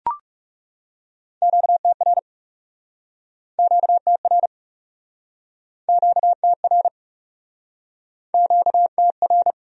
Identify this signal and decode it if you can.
Morse code, heard as a keyed tone, about 42 wpm, 700 Hz. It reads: QTR QTR QTR QTR